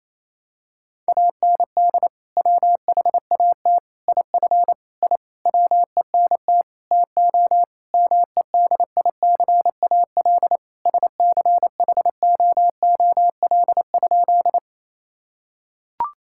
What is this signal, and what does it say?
Morse code, heard as a keyed tone, about 28 wpm, 700 Hz. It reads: AND WHAT IF I WENT TO MEDICAL SCHOOL?
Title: Sentences from Top 1000 Words - Review Part 5 - 28wpm